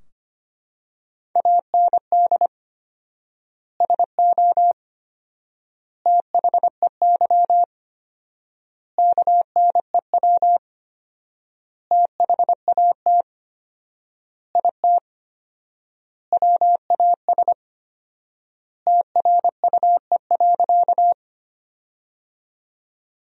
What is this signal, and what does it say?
Morse code, heard as a keyed tone, about 25 wpm, 700 Hz. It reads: AND SO THEY KNEW THAT IT WAS TRUE.
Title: 4x Word-Spacing - Sentences from Top 400 Words - 25wpm